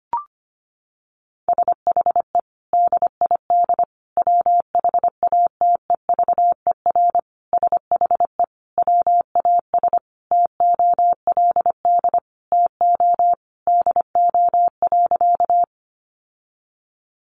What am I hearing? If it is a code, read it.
Morse code, heard as a keyed tone, about 25 wpm, 700 Hz. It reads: SHE DID WHATEVER SHE WAS TOLD TO DO.
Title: Sentences from Top 1100 Words 25wpm